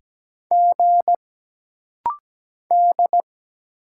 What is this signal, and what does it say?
Morse code, heard as a keyed tone, about 17 wpm, 700 Hz. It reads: G D
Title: Single Letter-Number - Rapid-Fire 17wpm